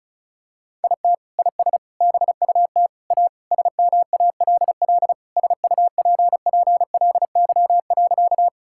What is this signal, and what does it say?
Morse code, heard as a keyed tone, about 34 wpm, 700 Hz. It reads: IT IS BUT A SMALL SUPPLY.
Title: Sentences from Top 1000 Words - Review Part 5 - 35wpm